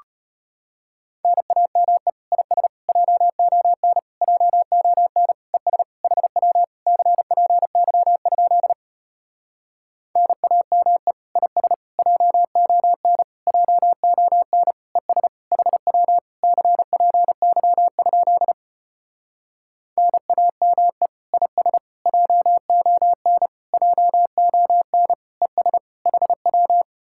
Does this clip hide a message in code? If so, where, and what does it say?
Yes: Morse code from 1.2 to 26.8 s, heard as a keyed tone, about 33 wpm, 700 Hz. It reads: NAME IS JON JON ES HW CPY? NAME IS JON JON ES HW CPY? NAME IS JON JON ES HW